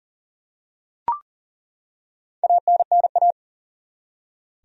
Morse code, heard as a keyed tone, about 40 wpm, 700 Hz. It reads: ANNA
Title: USA Common Names - Max 4 Letters - 40wpm